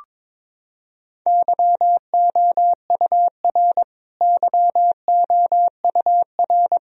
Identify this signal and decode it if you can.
Morse code, heard as a keyed tone, about 22 wpm, 700 Hz. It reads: YOUR YOUR
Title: Top 100 Words Repeated 22wpm